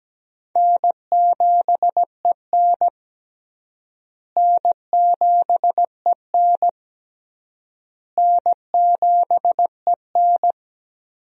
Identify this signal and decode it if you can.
Morse code, heard as a keyed tone, about 17 wpm, 700 Hz. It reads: N7EN N7EN N7EN
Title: USA Call Signs - 4 Characters - Repeated 3x - 17wpm